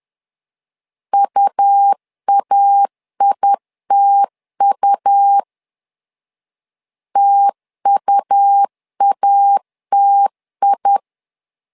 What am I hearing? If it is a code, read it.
Morse code, heard as a keyed tone, about 10 wpm, 800 Hz. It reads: UAITU TUATI
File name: LEZIONI / 2 UHDN / 2 U 10wpm 1622